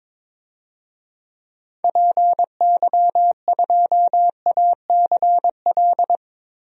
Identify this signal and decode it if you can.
Morse code, heard as a keyed tone, about 22 wpm, 700 Hz. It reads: PY2ACL
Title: DXCC-100 Call Signs - 6 Characters - 22wpm